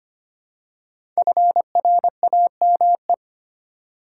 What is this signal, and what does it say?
Morse code, heard as a keyed tone, about 25 wpm, 700 Hz. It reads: FRAME
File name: 5 Letter Words - First 3 Letters Given - 25wpm